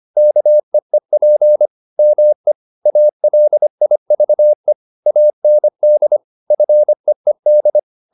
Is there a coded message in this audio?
Yes: Morse code from 0.2 to 7.8 s, heard as a keyed tone, about 25 wpm, 600 Hz. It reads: KEEP ME ALIVE AND FEED